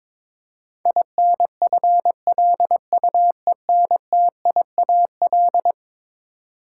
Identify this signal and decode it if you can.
Morse code, heard as a keyed tone, about 21 wpm, 700 Hz. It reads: INFLUENTIAL
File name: practice / 22wpm / Eleven Letter Words - 22wpm